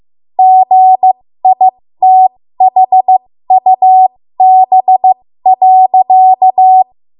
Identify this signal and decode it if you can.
Morse code, heard as a keyed tone, about 15 wpm, spaced as for 13 wpm, 750 Hz. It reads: GITHUB.
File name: Morse / 13WPM